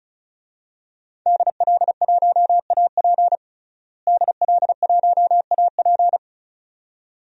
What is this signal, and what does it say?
Morse code, heard as a keyed tone, about 36 wpm, 700 Hz. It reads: DL1AP DL1AP
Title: DXCC-100 Call Signs - 5 Characters - Repeated 2x - 35wpm